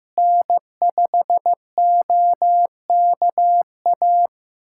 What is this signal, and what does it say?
Morse code, heard as a keyed tone, about 15 wpm, 700 Hz. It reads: N5OKA
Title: USA Callsign DE Callsign - 5 Characters - 15wpm